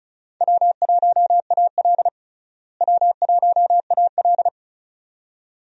Morse code, heard as a keyed tone, about 36 wpm, 700 Hz. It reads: W1AL W1AL
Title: USA Call Signs - 4 Characters - Repeated 2x - 35wpm